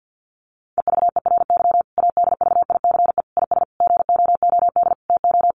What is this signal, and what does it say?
Morse code, heard as a keyed tone, about 50 wpm, 700 Hz. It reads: EVERY ADVICE IS GOOD TO
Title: Sentences from Top 1100 Words 50wpm